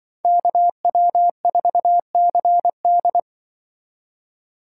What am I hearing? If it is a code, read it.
Morse code, heard as a keyed tone, about 24 wpm, 700 Hz. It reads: KW4CD
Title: Speed Racing - USA 2x2 Call Signs 20wpm